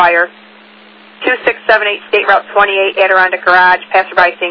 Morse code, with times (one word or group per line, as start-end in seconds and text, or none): none